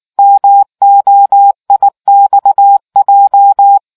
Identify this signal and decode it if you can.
Morse code, heard as a keyed tone, about 19 wpm, 800 Hz. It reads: MOIXJ